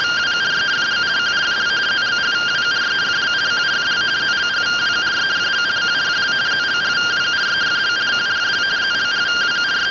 Two-tone teleprinter signal, about 170 Hz shift.